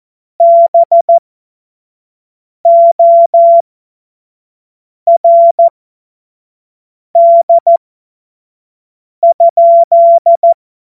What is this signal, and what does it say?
Morse code, heard as a keyed tone, about 14 wpm, spaced as for 5 wpm, 680 Hz. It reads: BORD?
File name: Morsekurs / Witze / Witz 20. Juni 2023 25 F70